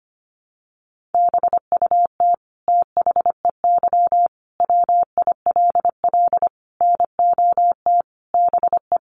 Morse code, heard as a keyed tone, about 25 wpm, 700 Hz. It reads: BUT THEY WILL NOT BE
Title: Sentences from Top 900 Words 25wpm